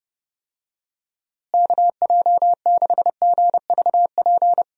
Morse code, heard as a keyed tone, about 30 wpm, 700 Hz. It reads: KJ6GVP
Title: USA Call Signs - 6 Characters - Repeated 3x - 30wpm